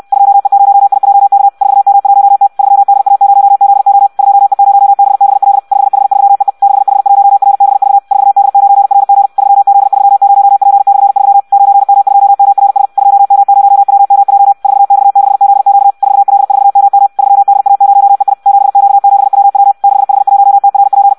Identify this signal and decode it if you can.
Morse code, heard as a keyed tone, about 64 wpm, 800 Hz. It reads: 2E0IJK 4M1T 3DA0ZO 2E0BBL 5H3EE 6H1RBV 4D9RG 3Z30P/3 9M2MRS 2M0WMJ 4X6ZQ 4L4MM 3DA0EI 7Q7WW 6H1TRP
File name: Audio practica CW / qrq-tx-66 2